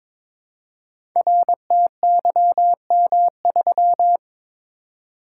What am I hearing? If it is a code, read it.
Morse code, heard as a keyed tone, about 22 wpm, 700 Hz. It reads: RTYM3
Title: Code Groups - 5 Letter:Numbers - 22wpm